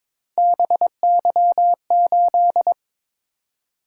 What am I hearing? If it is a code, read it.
Morse code, heard as a keyed tone, about 21 wpm, 700 Hz. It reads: BY8
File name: practice / 22wpm / Three Letter-Number Combinations - 22wpm